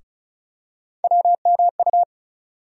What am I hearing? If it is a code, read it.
Morse code, heard as a keyed tone, about 36 wpm, 700 Hz. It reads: WMU
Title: USA Call Sign Postfixes - 3 Letters 35wpm